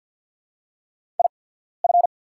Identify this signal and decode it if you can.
Morse code, heard as a keyed tone, about 50 wpm, spaced as for 14 wpm, 700 Hz. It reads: IF
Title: 8x Character Spacing - Top 100 Words - 50wpm